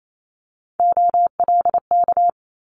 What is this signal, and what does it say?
Morse code, heard as a keyed tone, about 28 wpm, 700 Hz. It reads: OLK
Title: Multi-letter Phonemes Quick Fire 28wpm